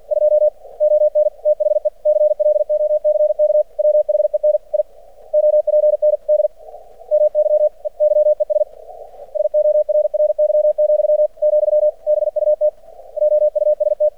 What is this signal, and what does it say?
Morse code, heard as a keyed tone, about 50 wpm, 600 Hz. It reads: 2 ON THE CLOCK WHEN I OPND MY YES SORRY, , BUT OUST
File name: archived / CRSnet